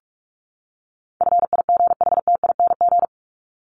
Morse code, heard as a keyed tone, about 44 wpm, 700 Hz. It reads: FIGHTING
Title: ING Suffix - Top 100 Words with Base Words 45wpm